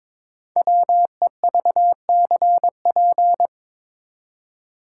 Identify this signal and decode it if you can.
Morse code, heard as a keyed tone, about 22 wpm, 700 Hz. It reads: WEVCP
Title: Code Groups - 5 Letter:Numbers - 22wpm